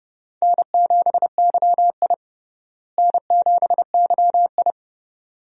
Morse code, heard as a keyed tone, about 30 wpm, 700 Hz. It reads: N7YI N7YI
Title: USA Call Signs - 4 Characters - Repeated 2x - 30wpm